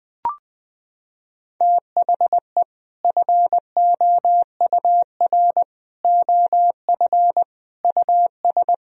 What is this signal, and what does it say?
Morse code, heard as a keyed tone, about 19 wpm, 700 Hz. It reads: THE FOUR OF US
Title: Sentences from Top 1000 Words - Review Part 1 - 20wpm